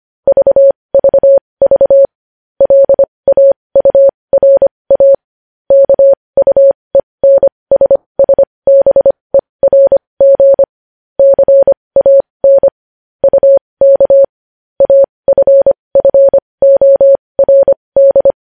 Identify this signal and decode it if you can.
Morse code, heard as a keyed tone, about 25 wpm, spaced as for 20 wpm, 550 Hz. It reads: VVV LAURA KUENSSBERG CAN UK AFFORD
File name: podcast / News headlines-1x-25x20